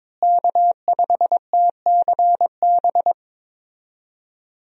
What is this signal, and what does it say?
Morse code, heard as a keyed tone, about 22 wpm, 700 Hz. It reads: K5TCB
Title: USA Call Signs - 5 Characters - Repeated 2x - 22wpm